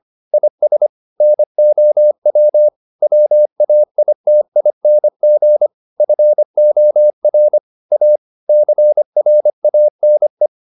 Morse code, heard as a keyed tone, about 25 wpm, 600 Hz. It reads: IS NOW WAITING FOR A CRANE